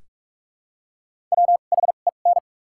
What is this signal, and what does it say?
Morse code, heard as a keyed tone, about 46 wpm, spaced as for 32 wpm, 700 Hz. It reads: WHEN
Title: 2x Character Spacing - Top 300 Words - 45wpm